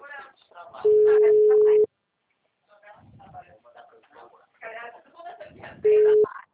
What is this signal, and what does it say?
Ringback tone (Europe, 425 Hz), with speech.